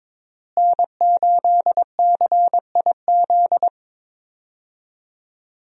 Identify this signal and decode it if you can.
Morse code, heard as a keyed tone, about 21 wpm, 700 Hz. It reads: N8CIZ
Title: USA Call Signs - 5 Characters - Repeated 2x - 22wpm